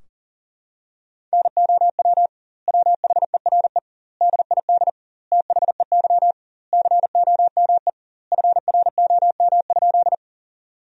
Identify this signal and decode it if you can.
Morse code, heard as a keyed tone, about 40 wpm, 700 Hz. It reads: NOW WHERE DID THEY COME FROM?
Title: 2x Word-Spacing - Sentences from Top 200 Words - 40wpm